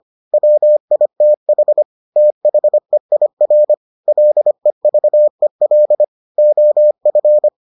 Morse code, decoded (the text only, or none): WITH THEIR LEVEL OF